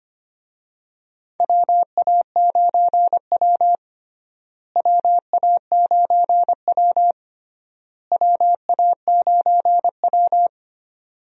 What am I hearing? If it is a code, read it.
Morse code, heard as a keyed tone, about 25 wpm, 700 Hz. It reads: WA9W WA9W WA9W
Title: USA Call Signs - 4 Characters - Repeated 3x - 25wpm